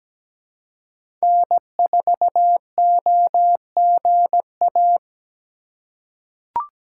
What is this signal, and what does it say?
Morse code, heard as a keyed tone, about 17 wpm, 700 Hz. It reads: N4OGA
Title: USA Call Signs - 5 Characters - 2x Character Spacing - 17wpm